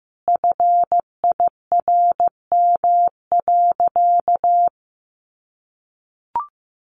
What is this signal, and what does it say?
Morse code, heard as a keyed tone, about 15 wpm, 700 Hz. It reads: FIRM.
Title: Sentences from Top 1400 Words 15wpm